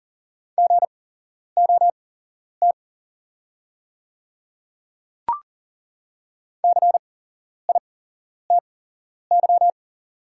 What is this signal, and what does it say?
Morse code, heard as a keyed tone, about 40 wpm, spaced as for 11 wpm, 700 Hz. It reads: GOT CITY
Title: 8x Character Spacing - Top 500 Words - 40wpm